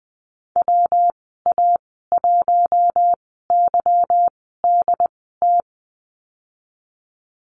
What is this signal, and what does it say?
Morse code, heard as a keyed tone, about 20 wpm, spaced as for 14 wpm, 700 Hz. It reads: WA1YDT
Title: 193 Lesson 31g - Letter v - Call Signs 20wpm